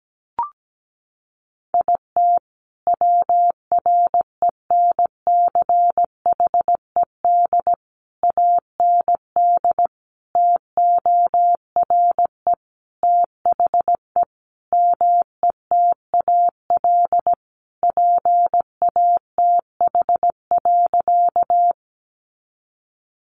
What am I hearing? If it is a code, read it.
Morse code, heard as a keyed tone, about 17 wpm, 700 Hz. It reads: IT WRENCHED AND TORE THE METAL PATH.
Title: A Sound Of Thunder 17wpm